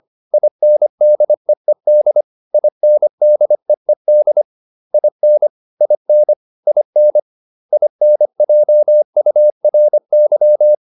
Morse code, heard as a keyed tone, about 25 wpm, 600 Hz. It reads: INDEED INDEED IN IN IN INJURY